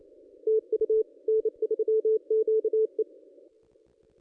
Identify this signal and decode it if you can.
Morse code, heard as a keyed tone, about 28 wpm, 440 Hz. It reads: TU N3QE
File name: contests / ss-cw-k5tr-op-n5rz -1400